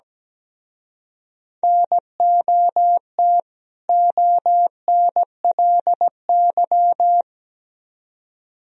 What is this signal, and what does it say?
Morse code, heard as a keyed tone, about 20 wpm, 700 Hz. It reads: NOT ONLY
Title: Speed-Racing - Sets of 2 Words 17wpm